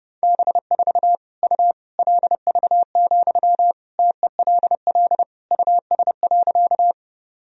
Morse code, heard as a keyed tone, about 30 wpm, 700 Hz. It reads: B4 U LV, TELL US.
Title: Sentences from Top 400 Words and Abbreviations 30wpm